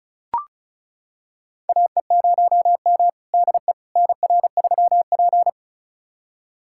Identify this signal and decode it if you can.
Morse code, heard as a keyed tone, about 36 wpm, 700 Hz. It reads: AE0M DE NR3P
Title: USA Callsign DE Callsign - 4 Characters - 35wpm